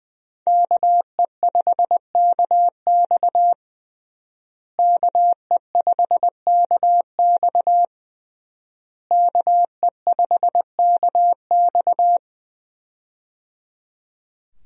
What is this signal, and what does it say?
Morse code, heard as a keyed tone, about 20 wpm, 700 Hz. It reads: KE5KX KE5KX KE5KX
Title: USA Call Signs - 5 Characters - Repeated 3x - 20wpm